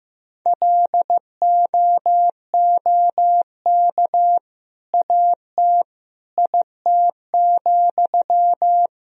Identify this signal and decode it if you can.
Morse code, heard as a keyed tone, about 15 wpm, 700 Hz. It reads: LOOK AT IT,
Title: Sentences from Top 200 Words and Abbreviations 15wpm